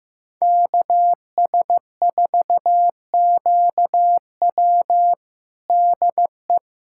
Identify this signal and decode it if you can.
Morse code, heard as a keyed tone, about 15 wpm, 700 Hz. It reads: KS4QW DE
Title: USA Callsign DE Callsign - 5 Characters - 15wpm